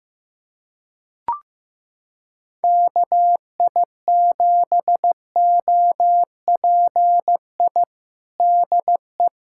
Morse code, heard as a keyed tone, about 15 wpm, 700 Hz. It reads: KI7OPI DE
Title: USA Callsign DE Callsign 15wpm